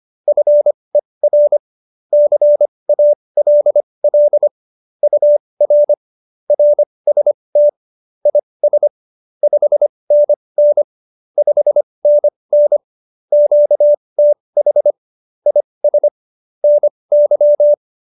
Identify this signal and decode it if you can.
Morse code, heard as a keyed tone, about 25 wpm, spaced as for 20 wpm, 590 Hz. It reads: FER CALL UR RST IS 5NN 5NN QTH IS NY
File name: QSO302 20